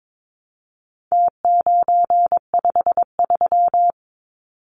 Morse code, heard as a keyed tone, about 22 wpm, 700 Hz. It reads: T953
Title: Numbers - Contest Style - 4-Digit 22wpm